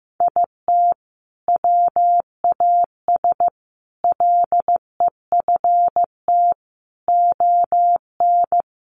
Morse code, heard as a keyed tone, about 15 wpm, 700 Hz. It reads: IT WAS LEFT ON
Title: Sentences from Top 700 Words 15wpm